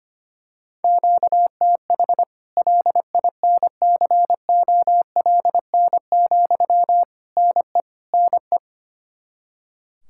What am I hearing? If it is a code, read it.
Morse code, heard as a keyed tone, about 25 wpm, 700 Hz. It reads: QTH LINCOLN, NE NE
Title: Practice QSO - Exchange 1 - Beginning - 25wpm